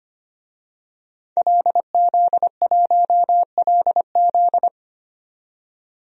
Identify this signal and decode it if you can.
Morse code, heard as a keyed tone, about 25 wpm, 700 Hz. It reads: LZ1LZ
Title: DXCC-100 Call Signs - 5 Characters - 25wpm